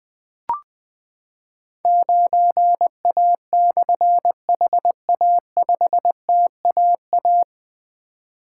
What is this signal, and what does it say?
Morse code, heard as a keyed tone, about 21 wpm, 700 Hz. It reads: 9A/HA5TAA
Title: Difficult Call Signs 20wpm